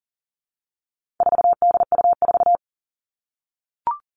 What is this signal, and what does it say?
Morse code, heard as a keyed tone, about 40 wpm, 700 Hz. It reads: 4DU4
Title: All Cut Numbers - 4-Digit 40wpm